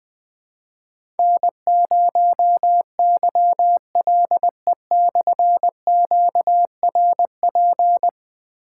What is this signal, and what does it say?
Morse code, heard as a keyed tone, about 20 wpm, 700 Hz. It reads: N0YLE/QRP
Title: Difficult Call Signs 20wpm